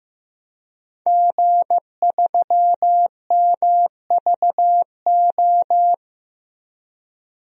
Morse code, heard as a keyed tone, about 15 wpm, 700 Hz. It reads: G3MVO
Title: DXCC-100 Call Signs - 5 Characters - Repeated 2x - 15wpm